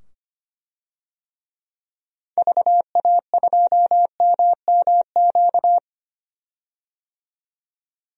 Morse code, heard as a keyed tone, about 25 wpm, 690 Hz. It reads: VA2MMQ
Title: DXCC-100 Call Signs - 6 Characters - 25wpm